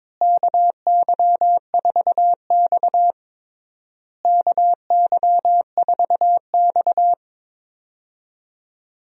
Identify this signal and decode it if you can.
Morse code, heard as a keyed tone, about 22 wpm, 700 Hz. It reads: KY4X KY4X
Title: USA Call Signs - 4 Characters - Repeated 3x - 22wpm